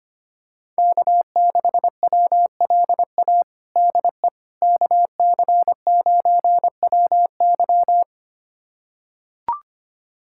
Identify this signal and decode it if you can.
Morse code, heard as a keyed tone, about 25 wpm, 700 Hz. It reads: K6WLA DE KC9WY
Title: USA Callsign DE Callsign 25wpm